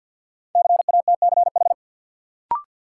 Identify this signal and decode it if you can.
Morse code, heard as a keyed tone, about 53 wpm, 700 Hz. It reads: CATCH
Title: Sentences from Top 1500 Words - New Word Drill Alt - 50wpm